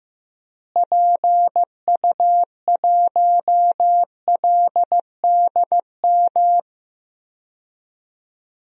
Morse code, heard as a keyed tone, about 15 wpm, 700 Hz. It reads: PU1LDM
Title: DXCC-100 Call Signs - 6 Characters - Repeated 3x - 15wpm